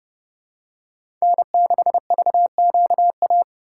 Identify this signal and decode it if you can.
Morse code, heard as a keyed tone, about 29 wpm, 700 Hz. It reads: N6VQA